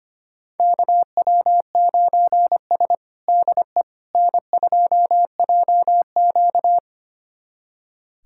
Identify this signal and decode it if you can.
Morse code, heard as a keyed tone, about 25 wpm, 700 Hz. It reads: KW9S DE N2JQ
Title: USA Callsign DE Callsign - 4 Characters - 25wpm